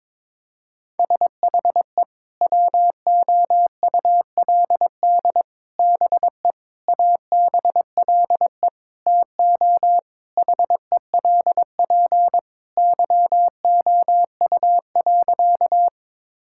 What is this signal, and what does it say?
Morse code, heard as a keyed tone, about 21 wpm, 700 Hz. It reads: SHE WOULD BE ABLE TO HELP YOU.
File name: practice / 22wpm / Sentences with 4-Word Phrases - 22wpm